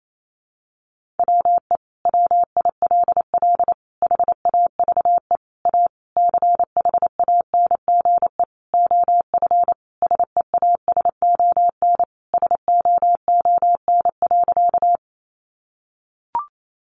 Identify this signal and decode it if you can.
Morse code, heard as a keyed tone, about 28 wpm, 700 Hz. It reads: WE WILL HAVE A CHANGE OF SEASON SOON.
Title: Sentences from Top 1400 Words 28wpm